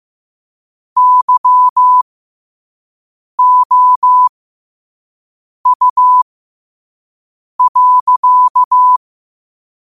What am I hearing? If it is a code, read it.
Morse code, heard as a keyed tone, about 15 wpm, spaced as for 5 wpm, 1000 Hz. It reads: YOU.
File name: qotdfiles / 5wpm